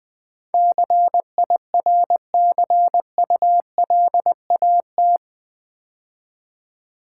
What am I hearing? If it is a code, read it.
Morse code, heard as a keyed tone, about 20 wpm, 700 Hz. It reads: CIRCULAT
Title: Build A Word - 4 Letters At A Time - 11-Letter Words - 20wpm